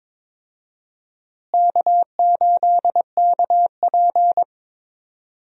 Morse code, heard as a keyed tone, about 22 wpm, 700 Hz. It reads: K8KP
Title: USA Call Signs - 4 Characters - Repeated 3x - 22wpm